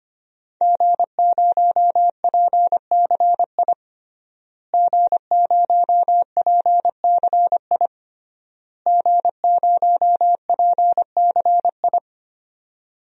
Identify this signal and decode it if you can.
Morse code, heard as a keyed tone, about 25 wpm, 700 Hz. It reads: G0PCI G0PCI G0PCI
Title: DXCC-100 Call Signs - 5 Characters - Repeated 3x - 25wpm